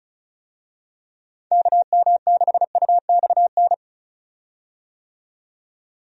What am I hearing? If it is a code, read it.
Morse code, heard as a keyed tone, about 34 wpm, 700 Hz. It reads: KM6UXN
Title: USA Call Signs - 6 Characters - 35wpm